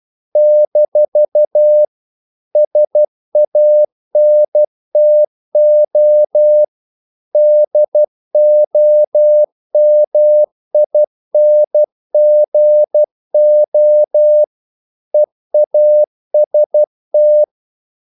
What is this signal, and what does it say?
Morse code, heard as a keyed tone, about 12 wpm, 600 Hz. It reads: - SANTO DOMINGO EAST
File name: CW / Practice / NationalCapitals12